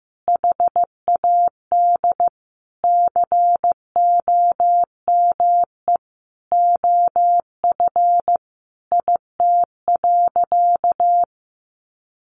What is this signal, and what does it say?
Morse code, heard as a keyed tone, about 14 wpm, 700 Hz. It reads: HAD COME OF IT.
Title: Sentences from Top 500 Words 15wpm